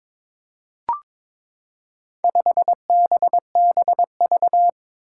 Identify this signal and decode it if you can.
Morse code, heard as a keyed tone, about 22 wpm, 700 Hz. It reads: 5BBV
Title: Four Challenging Letter-Number Combinations 22wpm